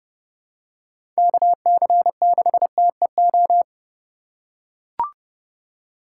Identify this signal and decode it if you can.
Morse code, heard as a keyed tone, about 32 wpm, 700 Hz. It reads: KC6TEO
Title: USA Call Signs - 6 Characters - 3x Character Spacing - 30wpm